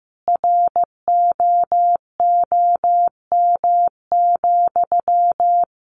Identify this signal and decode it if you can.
Morse code, heard as a keyed tone, about 15 wpm, 700 Hz. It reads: ROOM,
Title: Sentences from Top 1500 Words 15wpm